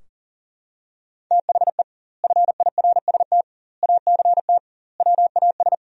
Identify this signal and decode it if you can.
Morse code, heard as a keyed tone, about 37 wpm, 700 Hz. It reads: THE FIRST ACT WAS
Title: 2x Word-Spacing - Sentences from Top 200 Words - 40wpm